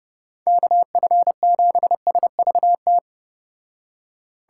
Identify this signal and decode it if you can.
Morse code, heard as a keyed tone, about 29 wpm, 700 Hz. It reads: KF7SVT